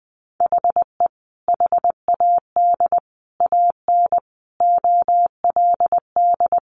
Morse code, heard as a keyed tone, about 20 wpm, 700 Hz. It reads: HE HAD AN OLD
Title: Sentences from Top 1100 Words 20wpm